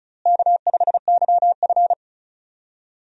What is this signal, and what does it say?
Morse code, heard as a keyed tone, about 36 wpm, 700 Hz. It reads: K5YF
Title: USA Call Signs - 4 Characters - Repeated 2x - 35wpm